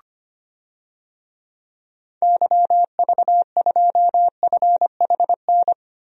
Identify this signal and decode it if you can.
Morse code, heard as a keyed tone, about 25 wpm, 700 Hz. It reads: YV2FHN
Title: DXCC-100 Call Signs - 6 Characters - Repeated 3x - 25wpm